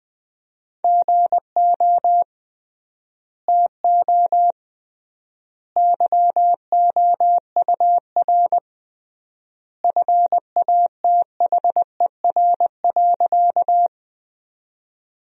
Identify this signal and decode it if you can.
Morse code, heard as a keyed tone, about 20 wpm, 700 Hz. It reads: GO TO YOUR FATHER.
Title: 3x Word-Spacing - Sentences from Top 300 Words - 20wpm